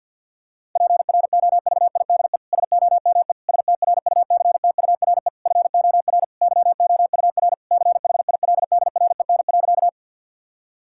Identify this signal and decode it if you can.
Morse code, heard as a keyed tone, about 50 wpm, 700 Hz. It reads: PROVIDE SOME STRUCTURE FOR YOUR CHILDREN.
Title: Sentences from Top 1000 Words - Review Part 6 - 50wpm